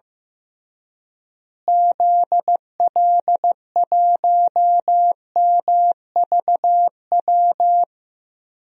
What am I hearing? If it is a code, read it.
Morse code, heard as a keyed tone, about 15 wpm, 700 Hz. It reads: ZL1MVW
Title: DXCC-100 Call Signs - 6 Characters - Repeated 3x - 15wpm